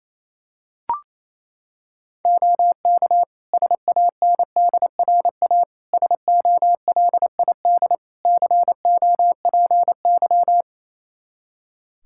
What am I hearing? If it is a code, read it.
Morse code, heard as a keyed tone, about 28 wpm, 700 Hz. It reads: OK SANDRA SOLID COPY
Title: Practice QSO - Exchange 3 - More Info - 28wpm